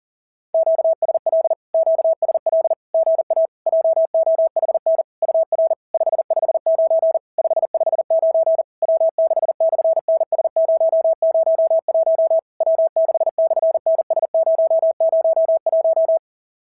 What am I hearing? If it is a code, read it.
Morse code, heard as a keyed tone, about 37 wpm, 660 Hz. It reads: QSL QSL GA JOHN UR 559 559 W6/NS001 W6/NS001
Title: SOTA QSO Practice 40wpm